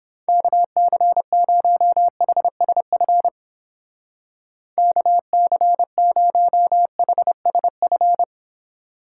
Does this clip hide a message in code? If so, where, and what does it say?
Yes: Morse code from 0.3 to 8.2 s, heard as a keyed tone, about 31 wpm, 700 Hz. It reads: KC0HSF KC0HSF